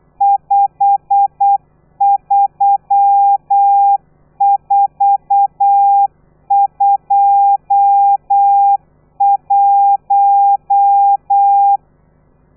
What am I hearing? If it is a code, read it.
Morse code, heard as a keyed tone, about 8 wpm, 800 Hz. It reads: 53421
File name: telegrafia / i2tfj / Lez12 35